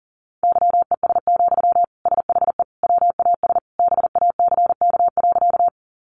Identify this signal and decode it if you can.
Morse code, heard as a keyed tone, about 40 wpm, 700 Hz. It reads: YES, SHE WAS BACK.
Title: Sentences from Top 500 Words 40wpm